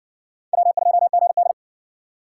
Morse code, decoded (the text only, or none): W2GD